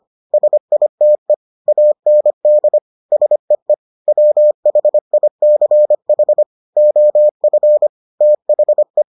Morse code, heard as a keyed tone, about 25 wpm, 600 Hz. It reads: SITE AND SEE WHICH OF THE